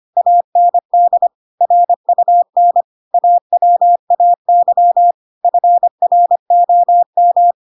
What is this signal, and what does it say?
Morse code, heard as a keyed tone, about 25 wpm, 700 Hz. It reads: AND RUN AWAY FROM
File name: Pinocchio / Pinocchio ch4 25 25 ik1hge